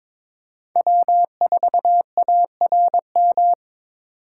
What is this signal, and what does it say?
Morse code, heard as a keyed tone, about 22 wpm, 700 Hz. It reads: W4ARM
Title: USA Call Signs - 5 Characters - 3x Character Spacing - 22wpm